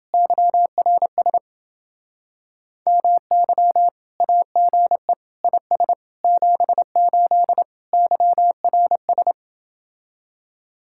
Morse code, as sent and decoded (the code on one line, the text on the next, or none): -.-- .-. ... / -- -.-- / .- --. / .. ... / --... ---.. / -.-- .-. ...
YRS MY AGE IS 78 YRS